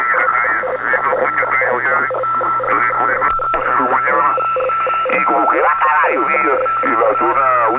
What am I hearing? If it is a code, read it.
Two-tone teleprinter signal, about 850 Hz shift, with speech.